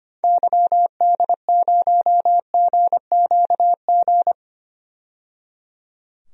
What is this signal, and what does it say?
Morse code, heard as a keyed tone, about 25 wpm, 700 Hz. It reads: YD0GQG
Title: DXCC-100 Call Signs - 6 Characters - 25wpm